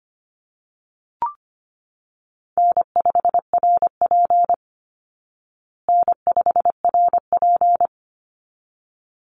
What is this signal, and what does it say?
Morse code, heard as a keyed tone, about 25 wpm, 700 Hz. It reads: N5RP N5RP
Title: Call Signs Repeated 3x - 25wpm